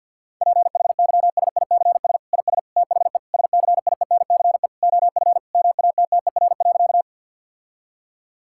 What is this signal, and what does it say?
Morse code, heard as a keyed tone, about 50 wpm, 700 Hz. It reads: PHYSICS IS THE SCIENCE OF MATTER.
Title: Sentences from Top 1000 Words - Review Part 5 - 50wpm